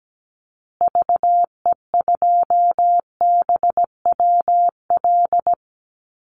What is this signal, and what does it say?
Morse code, heard as a keyed tone, about 17 wpm, 700 Hz. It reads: VE2BWL
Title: Call Signs Repeated 2x - 17wpm